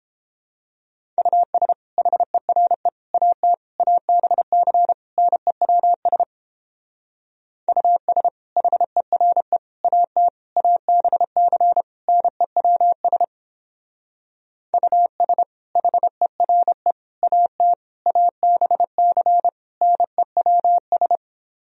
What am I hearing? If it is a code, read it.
Morse code, heard as a keyed tone, about 31 wpm, 700 Hz. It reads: US HERE AT ABC NEWS US HERE AT ABC NEWS US HERE AT ABC NEWS